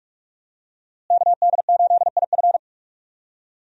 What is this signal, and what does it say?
Morse code, heard as a keyed tone, about 41 wpm, 700 Hz. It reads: KD8IF
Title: USA Call Signs - 5 Characters - Repeated 2x - 45wpm